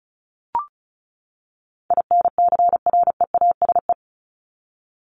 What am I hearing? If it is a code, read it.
Morse code, heard as a keyed tone, about 34 wpm, 700 Hz. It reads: INCREASE
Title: LY Suffix - Top 100 Words with Base Words 35wpm